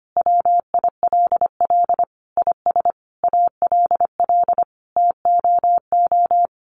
Morse code, heard as a keyed tone, about 25 wpm, 700 Hz. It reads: WILL IS ALL TOO
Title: Sentences from Top 1500 Words 25wpm